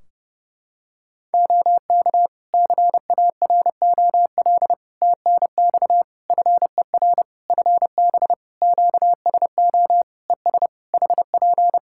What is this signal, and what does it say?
Morse code, heard as a keyed tone, about 29 wpm, 700 Hz. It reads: OK CAROL TNX FER FB QSO ES HP
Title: Practice QSO - Exchange 4 - Ending - 30wpm